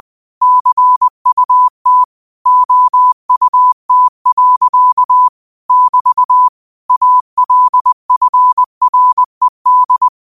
Morse code, heard as a keyed tone, about 20 wpm, 1000 Hz. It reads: CUT OUT. <BT> ALFRED